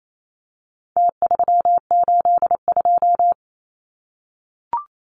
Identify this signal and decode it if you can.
Morse code, heard as a keyed tone, about 28 wpm, 700 Hz. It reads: T382
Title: Numbers - Contest Style - 4-Digit 28wpm